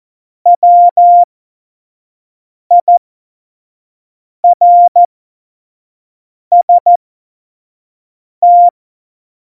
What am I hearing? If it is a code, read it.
Morse code, heard as a keyed tone, about 14 wpm, spaced as for 5 wpm, 700 Hz. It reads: WIRST
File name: Morsekurs / Witze / Witz 20 March 2018, 25 F70